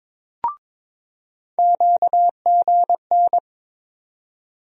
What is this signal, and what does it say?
Morse code, heard as a keyed tone, about 22 wpm, 700 Hz. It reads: QGN
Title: USA Call Sign Postfixes - 3 Letters 22wpm